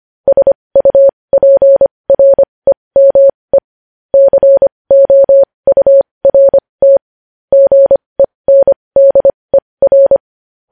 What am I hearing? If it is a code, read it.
Morse code, heard as a keyed tone, about 25 wpm, spaced as for 20 wpm, 550 Hz. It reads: SUPREME COURT GENDER
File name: podcast / News headlines-1x-25x20